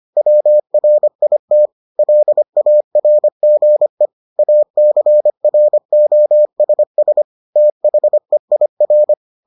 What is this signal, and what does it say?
Morse code, heard as a keyed tone, about 25 wpm, 600 Hz. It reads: WRIT LARGE ACROSS THEIR